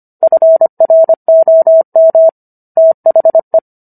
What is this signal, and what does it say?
Morse code, heard as a keyed tone, about 25 wpm, 650 Hz. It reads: FROM THE